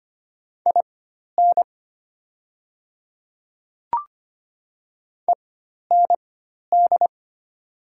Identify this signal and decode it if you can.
Morse code, heard as a keyed tone, about 25 wpm, spaced as for 12 wpm, 700 Hz. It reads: IN END
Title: 4x Character Spacing - Top 500 Words - 25wpm